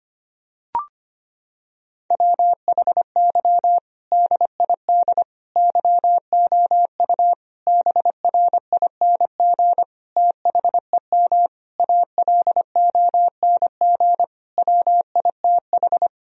Morse code, heard as a keyed tone, about 25 wpm, 700 Hz. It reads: WHY DID YOU BRING THEM ALONG WITH
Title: Sentences from Top 1000 Words - Review Part 3 - 25wpm